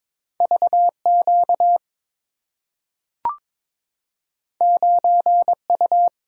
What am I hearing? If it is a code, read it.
Morse code, heard as a keyed tone, about 22 wpm, 700 Hz. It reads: VQ 9U
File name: practice / 22wpm / Two Letter-Number Combinations - 22wpm